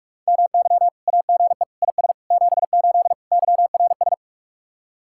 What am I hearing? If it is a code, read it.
Morse code, heard as a keyed tone, about 66 wpm, spaced as for 55 wpm, 700 Hz. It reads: MY AGE IS 78 YRS
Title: Speed-Racing Common QSO Phrases - 45wpm